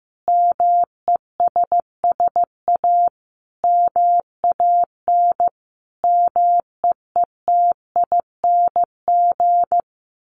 Morse code, heard as a keyed tone, about 15 wpm, 700 Hz. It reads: MESS A MAN MEETING
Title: A Sound Of Thunder 15wpm